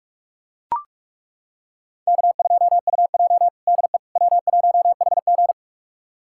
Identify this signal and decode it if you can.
Morse code, heard as a keyed tone, about 41 wpm, 700 Hz. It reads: K2UJ DE W1HG
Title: USA Callsign DE Callsign - 4 Characters - 45wpm